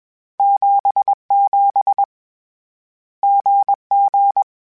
Morse code, heard as a keyed tone, about 21 wpm, 800 Hz. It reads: 77 GG
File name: Difficult Combos / Difficult Combos-mp3s / GZ GZ7 @20